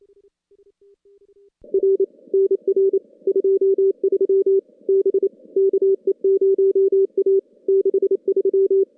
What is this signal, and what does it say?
Morse code, heard as a keyed tone, about 29 wpm, 410 Hz. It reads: R NR 23 B KE0A 63
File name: contests / CW-SS-k5tr- -2100